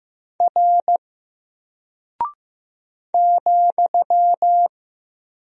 Morse code, heard as a keyed tone, about 15 wpm, 700 Hz. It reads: R ,
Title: Single Character (Letter:Number:Punctuation:Prosign) Alt 15wpm